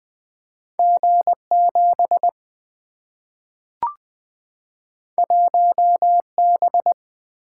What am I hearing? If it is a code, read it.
Morse code, heard as a keyed tone, about 20 wpm, 700 Hz. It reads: G 7 1 B
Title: Single-Letter Single-Number 20wpm